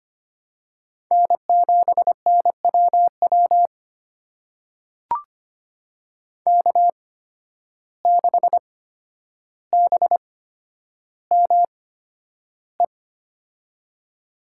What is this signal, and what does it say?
Morse code, heard as a keyed tone, about 25 wpm, spaced as for 11 wpm, 700 Hz. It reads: N7NWW K6BME
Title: USA Call Signs - 5 Characters - 8x Character Spacing - 25wpm